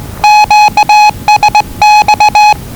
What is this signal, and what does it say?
Morse code, heard as a keyed tone, about 20 wpm, 840 Hz. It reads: QSX
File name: WellingtonRadioZlwCloseDownOnHfAndMf / 01-Zlw 4277khz 417.5 qrt until 0925z